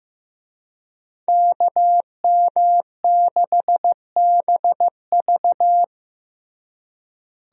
Morse code, heard as a keyed tone, about 16 wpm, 700 Hz. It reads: KM6BV
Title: USA Call Signs - 5 Characters - Repeated 2x - 15wpm